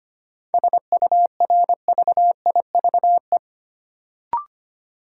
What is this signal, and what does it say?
Morse code, heard as a keyed tone, about 25 wpm, 700 Hz. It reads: SURVIVE
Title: Sentences from Top 1300 Words - New Word Drill - 25wpm